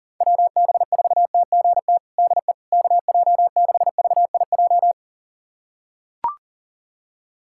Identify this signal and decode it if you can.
Morse code, heard as a keyed tone, about 40 wpm, 700 Hz. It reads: WB4TGT DE KJ6VIJ
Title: USA Callsign DE Callsign - 6 Characters - 40wpm